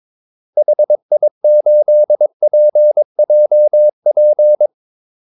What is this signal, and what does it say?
Morse code, heard as a keyed tone, about 22 wpm, 600 Hz. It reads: HI8PJP
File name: Callsigns-22wpm